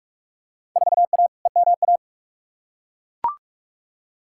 Morse code, heard as a keyed tone, about 42 wpm, 700 Hz. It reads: 4A EMA
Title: Field Day Exchanges - 45wpm